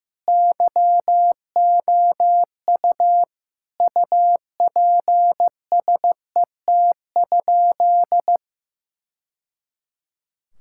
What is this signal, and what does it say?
Morse code, heard as a keyed tone, about 15 wpm, 700 Hz. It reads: YOU UPSET?
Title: Sentences from Top 1000 Words - Review Part 5 - 15wpm